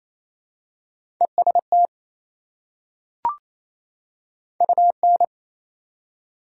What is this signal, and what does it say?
Morse code, heard as a keyed tone, about 28 wpm, 700 Hz. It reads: EST UN
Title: Prefixes and Suffixes 28wpm